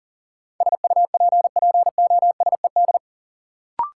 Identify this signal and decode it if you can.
Morse code, heard as a keyed tone, about 37 wpm, 700 Hz. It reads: SUPPOSED
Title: Sets of 3 Words - New Word Drill - 40wpm